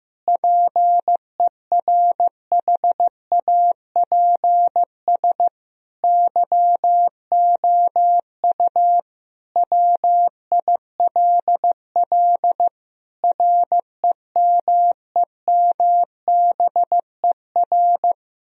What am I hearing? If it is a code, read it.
Morse code, heard as a keyed tone, about 15 wpm, 700 Hz. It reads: PERHAPS YOU WILL REMEMBER
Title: Sentences from Top 1000 Words - Review Part 1 - 15wpm